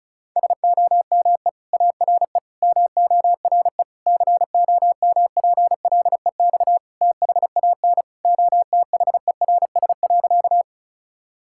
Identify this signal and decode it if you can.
Morse code, heard as a keyed tone, about 36 wpm, 700 Hz. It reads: SOME ARE MORE COMPLEX THAN OTHERS.
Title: Sentences from Top 1000 Words - Review Part 6 - 35wpm